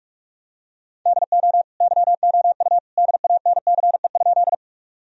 Morse code, heard as a keyed tone, about 46 wpm, 700 Hz. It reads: DO YOU DANCE?
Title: Sentences from Top 1000 Words - Review Part 4 - 45wpm